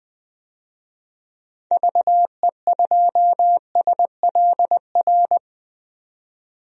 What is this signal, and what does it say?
Morse code, heard as a keyed tone, about 20 wpm, 700 Hz. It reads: VE2SLR
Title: DXCC-100 Call Signs - 6 Characters - 20wpm